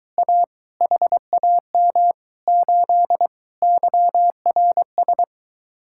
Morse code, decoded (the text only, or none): A HAM 8 YRS